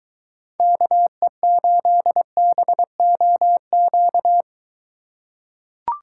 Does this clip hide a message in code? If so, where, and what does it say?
Yes: Morse code from 0.6 to 4.4 s, heard as a keyed tone, about 22 wpm, 700 Hz. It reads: KE8BOQ